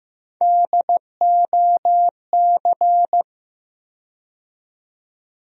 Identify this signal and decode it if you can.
Morse code, heard as a keyed tone, about 16 wpm, 700 Hz. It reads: DOC
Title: USA Call Sign Postfixes - 3 Letters 15wpm